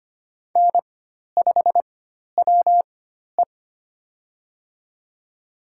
Morse code, heard as a keyed tone, about 25 wpm, spaced as for 12 wpm, 700 Hz. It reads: N5WE
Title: USA Call Signs - 4 Characters - 4x Character Spacing - 25wpm